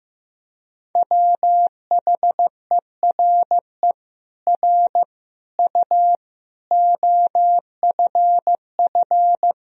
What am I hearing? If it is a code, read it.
Morse code, heard as a keyed tone, about 15 wpm, 700 Hz. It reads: WHERE R U OFF
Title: Sentences from Top 200 Words and Abbreviations 15wpm